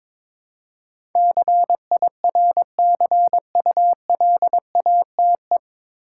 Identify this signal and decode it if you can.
Morse code, heard as a keyed tone, about 22 wpm, 700 Hz. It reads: CIRCULATE
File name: 9 Letter Words - First 3 Letters Given - 22wpm